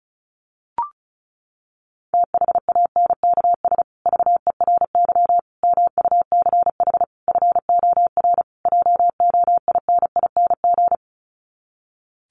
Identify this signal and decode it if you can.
Morse code, heard as a keyed tone, about 34 wpm, 700 Hz. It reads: THANKS VERY MUCH FOR JOINING
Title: Sets of 5 Words Encore2 35wpm